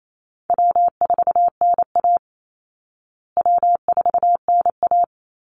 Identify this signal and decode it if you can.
Morse code, heard as a keyed tone, about 28 wpm, 700 Hz. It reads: W4NA W4NA
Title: Call Signs Repeated 3x - 28wpm